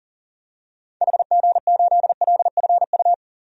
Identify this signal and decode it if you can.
Morse code, heard as a keyed tone, about 37 wpm, 700 Hz. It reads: HG8LFU
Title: DXCC-100 Call Signs - 6 Characters - 40wpm